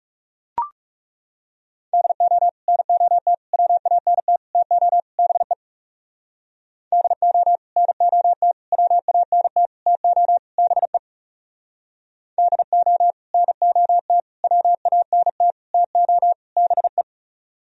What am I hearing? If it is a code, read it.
Morse code, heard as a keyed tone, about 38 wpm, 700 Hz. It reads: DO NOT WANT TO BE DO NOT WANT TO BE DO NOT WANT TO BE